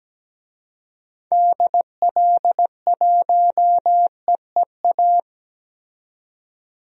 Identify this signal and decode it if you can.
Morse code, heard as a keyed tone, about 17 wpm, 700 Hz. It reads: DL1EEA
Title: DXCC-100 Call Signs - 6 Characters - 17wpm